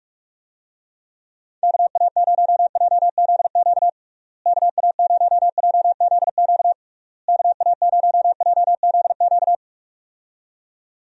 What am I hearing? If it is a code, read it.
Morse code, heard as a keyed tone, about 41 wpm, 700 Hz. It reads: KA0JZQ KA0JZQ KA0JZQ
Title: USA Call Signs - 6 Characters - Repeated 3x - 45wpm